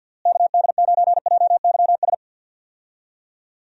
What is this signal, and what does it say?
Morse code, heard as a keyed tone, about 52 wpm, 700 Hz. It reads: KD9JYS
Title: USA Call Signs - 6 Characters - Repeated 3x - 50wpm